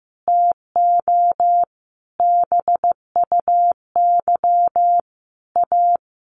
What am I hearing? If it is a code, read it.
Morse code, heard as a keyed tone, about 15 wpm, 700 Hz. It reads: TO BUY A
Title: Sentences from Top 1400 Words 15wpm